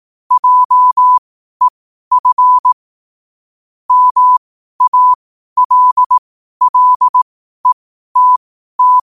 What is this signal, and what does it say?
Morse code, heard as a keyed tone, about 18 wpm, spaced as for 12 wpm, 1000 Hz. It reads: JEF MALLETT
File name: 12wpm